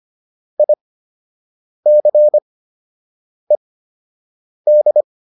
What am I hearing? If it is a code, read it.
Morse code, heard as a keyed tone, about 25 wpm, spaced as for 7 wpm, 600 Hz. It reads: ICED